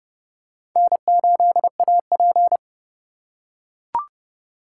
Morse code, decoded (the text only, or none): N8AP